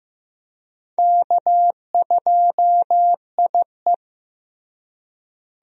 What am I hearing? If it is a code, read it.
Morse code, heard as a keyed tone, about 15 wpm, 700 Hz. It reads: K2IE
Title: USA Call Signs - 4 Characters - Repeated 3x - 15wpm